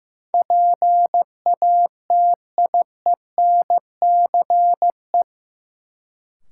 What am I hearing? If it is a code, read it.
Morse code, heard as a keyed tone, about 15 wpm, 700 Hz. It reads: PATIENCE
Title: Sentences from Top 1300 Words - New Word Drill Alt - 15wpm